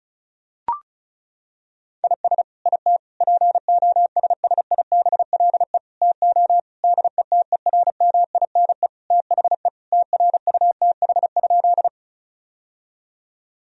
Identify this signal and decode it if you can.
Morse code, heard as a keyed tone, about 36 wpm, 700 Hz. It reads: IS IT POSSIBLE TO DETERMINE THE TRUTH?
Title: Sentences from Top 1000 Words - Review Part 4 - 35wpm